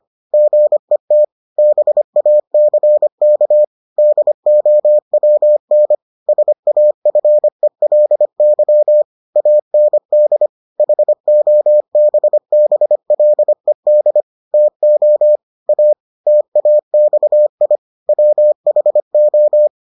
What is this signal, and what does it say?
Morse code, heard as a keyed tone, about 25 wpm, 600 Hz. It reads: GET BACK DOWN SAFELY AND HOBBLED TO A TAXI WHO